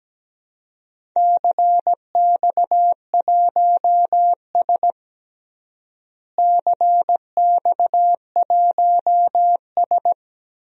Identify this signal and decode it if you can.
Morse code, heard as a keyed tone, about 17 wpm, 700 Hz. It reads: CX1S CX1S
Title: DXCC-100 Call Signs - 4 Characters - Repeated 3x - 17wpm